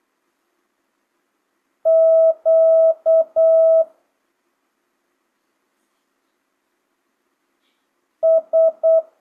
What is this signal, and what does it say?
Morse code, heard as a keyed tone, about 8 wpm, 650 Hz. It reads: Q S
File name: jh8cbh / 2025cw